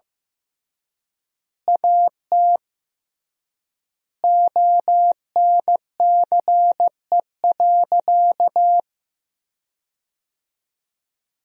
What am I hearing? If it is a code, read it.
Morse code, heard as a keyed tone, about 15 wpm, 700 Hz. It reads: AT ONCE.
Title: 3x Word-Spacing - Sentences from Top 300 Words - 15wpm